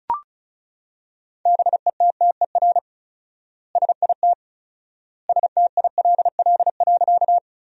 Morse code, heard as a keyed tone, about 36 wpm, 700 Hz. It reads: BETTER SIT STILL.
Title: 4x Word-Spacing - Sentences from Top 500 Words - 35wpm